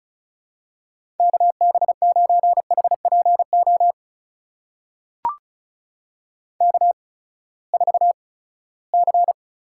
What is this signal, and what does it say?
Morse code, heard as a keyed tone, about 36 wpm, spaced as for 20 wpm, 700 Hz. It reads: KB9HPO K4C
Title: USA Call Signs - 8x Character Spacing - 35wpm